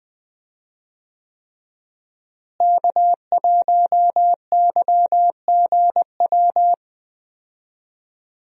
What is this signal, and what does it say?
Morse code, heard as a keyed tone, about 20 wpm, 700 Hz. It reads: K1YGW
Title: USA Call Signs - 5 Characters - Repeated 3x - 20wpm